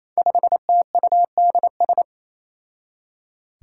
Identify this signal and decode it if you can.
Morse code, heard as a keyed tone, about 28 wpm, 700 Hz. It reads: 5TUDS
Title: Five Challenging Letter-Number Combinations 28wpm